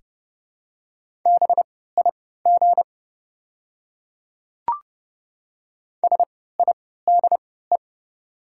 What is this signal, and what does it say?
Morse code, heard as a keyed tone, about 29 wpm, spaced as for 17 wpm, 690 Hz. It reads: BIG SIDE
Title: 3x Character Spacing - Top 300 Words - 30wpm